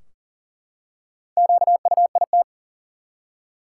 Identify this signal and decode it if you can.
Morse code, heard as a keyed tone, about 40 wpm, 700 Hz. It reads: QUIT
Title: Four Letter Words - 40wpm